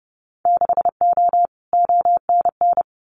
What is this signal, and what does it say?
Morse code, heard as a keyed tone, about 30 wpm, 700 Hz. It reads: 6O ONN
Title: Winter Field Day Exchanges Encore 30wpm